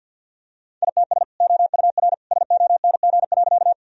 Morse code, heard as a keyed tone, about 51 wpm, 700 Hz. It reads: ITS OUR SONG.
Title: Sentences from Top 1000 Words - Review Part 2 - 50wpm